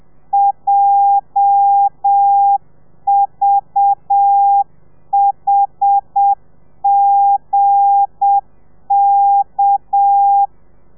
Morse code, heard as a keyed tone, about 7 wpm, 800 Hz. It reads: JVHGK
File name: telegrafia / i2tfj / Lez8 35